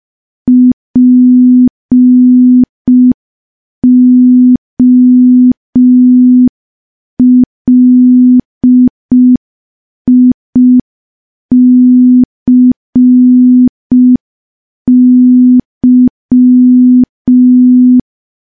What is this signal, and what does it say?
Morse code, heard as a keyed tone, about 5 wpm, 250 Hz. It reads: POLICY
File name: morse / morse-training-05WPM